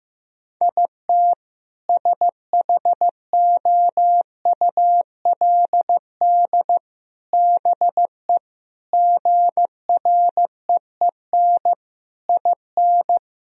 Morse code, heard as a keyed tone, about 15 wpm, 700 Hz. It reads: IT SHOULD BE GREEN IN
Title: Sentences from Top 1000 Words - Review Part 2 - 15wpm